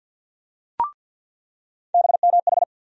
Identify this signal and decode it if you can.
Morse code, heard as a keyed tone, about 49 wpm, 700 Hz. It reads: BMH
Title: USA Call Sign Postfixes - 3 Letters 50wpm